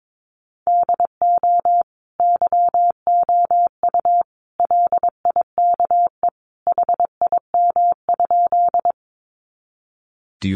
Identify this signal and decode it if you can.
Morse code, heard as a keyed tone, about 22 wpm, 700 Hz. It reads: DO YOU LIKE HIM?
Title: Sentences from Top 100 Words 22wpm